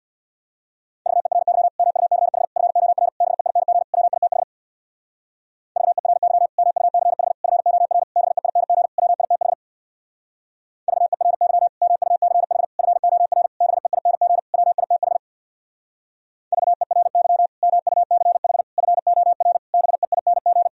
Morse code, heard as a keyed tone, about 64 wpm, 700 Hz. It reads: VERY MUCH FOR BEING WITH VERY MUCH FOR BEING WITH VERY MUCH FOR BEING WITH VERY MUCH FOR BEING